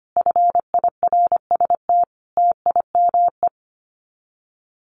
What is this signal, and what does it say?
Morse code, heard as a keyed tone, about 25 wpm, 700 Hz. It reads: FIRST TIME
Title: Sets of 5 Words Encore 25wpm